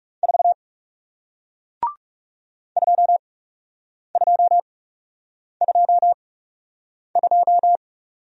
Morse code, heard as a keyed tone, about 39 wpm, spaced as for 8 wpm, 700 Hz. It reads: 4 2222